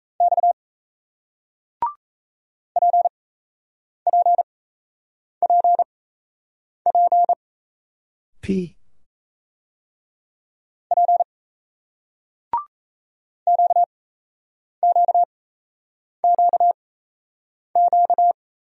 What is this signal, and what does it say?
Morse code, heard as a keyed tone, about 36 wpm, spaced as for 8 wpm, 700 Hz. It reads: X PPPP P QQQQ